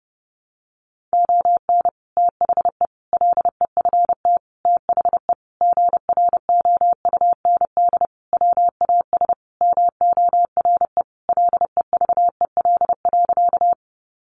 Morse code, heard as a keyed tone, about 30 wpm, 700 Hz. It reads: ON THE LEFT THE GROUND WAS MORE LEVEL.
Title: Sentences from Top 700 Words 30wpm